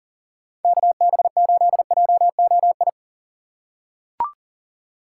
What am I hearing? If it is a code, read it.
Morse code, heard as a keyed tone, about 40 wpm, 700 Hz. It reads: KB8JOI
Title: USA Call Signs - 6 Characters - 6x Character Spacing - 40wpm